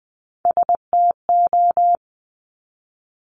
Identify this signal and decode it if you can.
Morse code, heard as a keyed tone, about 20 wpm, 700 Hz. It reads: STO
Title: Top Three Letter Combinations in Words 20wpm